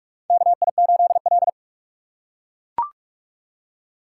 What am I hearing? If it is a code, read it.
Morse code, heard as a keyed tone, about 44 wpm, 700 Hz. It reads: KI8L
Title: USA Call Signs - 4 Characters - 3x Character Spacing - 45wpm